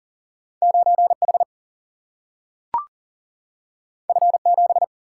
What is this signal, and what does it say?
Morse code, heard as a keyed tone, about 38 wpm, 700 Hz. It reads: 9 H F 7
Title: Single-Letter Single-Number 40wpm